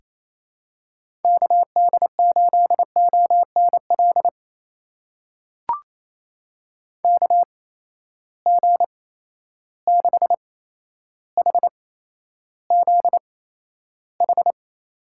Morse code, heard as a keyed tone, about 28 wpm, spaced as for 12 wpm, 700 Hz. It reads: KD8ONL KG6HZH
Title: USA Call Signs - 6 Characters - 8x Character Spacing - 28wpm